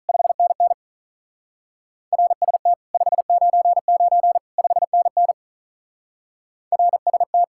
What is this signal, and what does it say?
Morse code, heard as a keyed tone, about 47 wpm, 700 Hz. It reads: 5NN RST 599 5NN RST